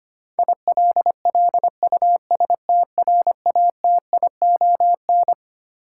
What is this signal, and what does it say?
Morse code, heard as a keyed tone, about 25 wpm, 700 Hz. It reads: ILLUSTRATION
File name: Twelve Letter Words - 25wpm